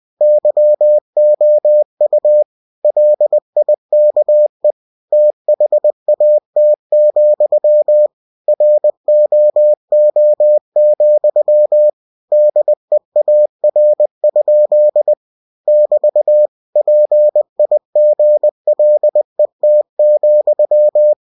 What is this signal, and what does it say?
Morse code, heard as a keyed tone, about 20 wpm, 600 Hz. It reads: YOU LIKE THAT, ROO, DEAR? <BT> PIGLET,
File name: cw-books / winnie-the-pooh / winnie-20wpm-page-0071